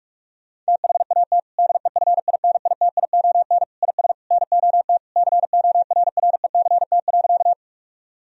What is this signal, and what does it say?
Morse code, heard as a keyed tone, about 46 wpm, 700 Hz. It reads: THAT DEFINITION IS NOT CORRECT.
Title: Sentences from Top 1000 Words - Review Part 4 - 45wpm